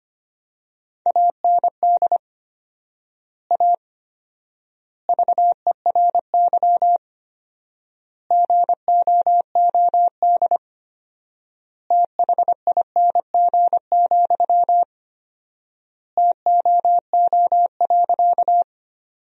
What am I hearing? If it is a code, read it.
Morse code, heard as a keyed tone, about 25 wpm, 700 Hz. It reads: AND A VERY GOOD THING, TOO.
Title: 4x Word-Spacing - Sentences from Top 200 Words - 25wpm